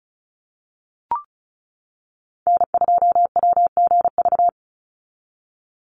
Morse code, heard as a keyed tone, about 34 wpm, 700 Hz. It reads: N2WGV
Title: USA Call Signs 35wpm